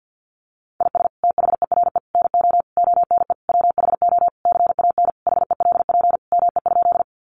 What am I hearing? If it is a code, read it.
Morse code, heard as a keyed tone, about 50 wpm, 700 Hz. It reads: IS THERE NO ONE WHO CAN HELP ME?
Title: Sentences from Top 200 Words 50wpm